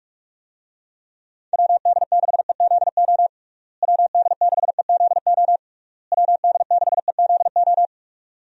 Morse code, heard as a keyed tone, about 46 wpm, 700 Hz. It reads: WD6EZO WD6EZO WD6EZO
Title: USA Call Signs - 6 Characters - Repeated 3x - 45wpm